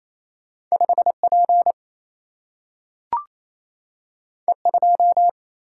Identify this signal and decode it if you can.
Morse code, heard as a keyed tone, about 28 wpm, 700 Hz. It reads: 5P E2
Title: Single-Letter Single-Number 28wpm